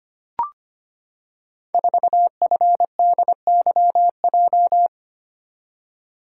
Morse code, heard as a keyed tone, about 25 wpm, 700 Hz. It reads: 4FDYJ
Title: Code Groups - 5 Letter:Numbers - 25wpm